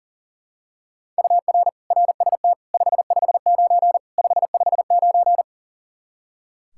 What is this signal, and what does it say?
Morse code, heard as a keyed tone, about 40 wpm, 700 Hz. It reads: UR RST 559 559
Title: Practice QSO - Exchange 1 - Beginning - 40wpm